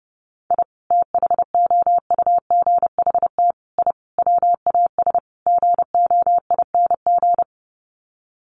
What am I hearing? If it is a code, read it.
Morse code, heard as a keyed tone, about 30 wpm, 700 Hz. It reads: I THOUGHT I WAS GOING
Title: Sets of 5 Words Encore2 30wpm